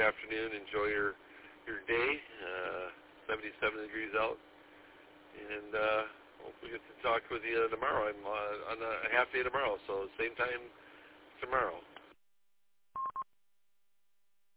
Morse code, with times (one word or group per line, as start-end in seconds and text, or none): none